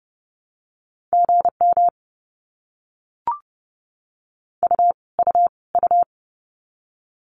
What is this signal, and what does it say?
Morse code, heard as a keyed tone, about 30 wpm, 700 Hz. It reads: GM U U U
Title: Abbreviations Repeated 3x - 30wpm